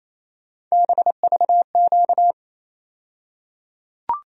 Morse code, heard as a keyed tone, about 28 wpm, 700 Hz. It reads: BVQ
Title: USA Call Sign Postfixes - 3 Letters 28wpm